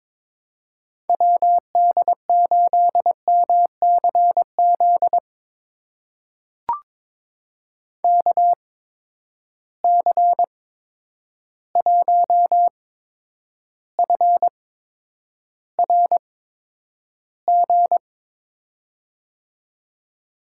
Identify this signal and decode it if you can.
Morse code, heard as a keyed tone, about 22 wpm, spaced as for 9 wpm, 700 Hz. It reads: WD8MCZ KC1FRG
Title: USA Call Signs - 6 Characters - 8x Character Spacing - 22wpm